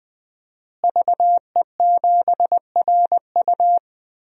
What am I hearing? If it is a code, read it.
Morse code, heard as a keyed tone, about 20 wpm, 700 Hz. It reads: VE7RU
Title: DXCC-100 Call Signs - 5 Characters - Repeated 2x - 20wpm